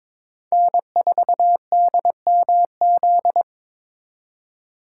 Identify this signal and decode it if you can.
Morse code, heard as a keyed tone, about 26 wpm, 700 Hz. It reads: N4DMZ
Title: Speed Racing - USA 1x3 Call Signs 22wpm